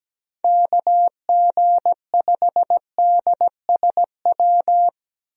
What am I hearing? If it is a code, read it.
Morse code, heard as a keyed tone, about 17 wpm, 700 Hz. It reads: KG5DSW
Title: USA Call Signs - 6 Characters - 17wpm